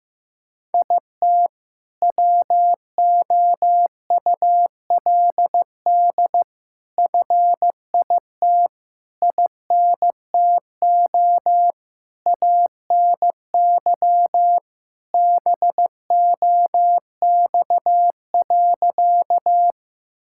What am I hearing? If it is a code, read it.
Morse code, heard as a keyed tone, about 15 wpm, 700 Hz. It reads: IT WOULD FIT INTO ANY BOX.
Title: Sentences from Top 1000 Words - Review Part 5 - 15wpm